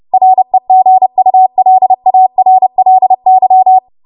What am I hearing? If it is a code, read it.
Morse code, heard as a keyed tone, about 30 wpm, 750 Hz. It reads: REGULARLY
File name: Morse / 30WPM